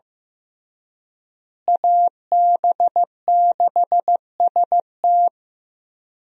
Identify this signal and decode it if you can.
Morse code, heard as a keyed tone, about 15 wpm, 700 Hz. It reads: AB6ST
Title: USA Call Signs - 5 Characters - Repeated 3x - 15wpm